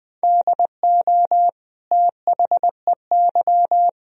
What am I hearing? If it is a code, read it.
Morse code, heard as a keyed tone, about 20 wpm, 700 Hz. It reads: DO THEY
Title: Sentences from Top 1000 Words - Review Part 2 - 20wpm